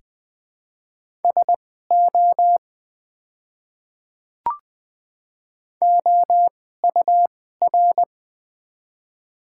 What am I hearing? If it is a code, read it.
Morse code, heard as a keyed tone, about 20 wpm, spaced as for 14 wpm, 700 Hz. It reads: SO OUR
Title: 2x Character Spacing - Top 200 Words - 20wpm